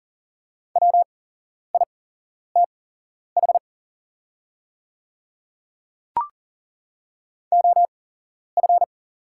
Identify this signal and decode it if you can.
Morse code, heard as a keyed tone, about 38 wpm, spaced as for 11 wpm, 690 Hz. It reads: WITH OF
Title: 8x Character Spacing - Top 200 Words - 40wpm